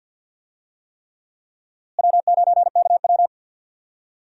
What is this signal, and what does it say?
Morse code, heard as a keyed tone, about 50 wpm, 700 Hz. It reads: W9KW
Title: USA Call Signs - 4 Characters - Repeated 3x - 50wpm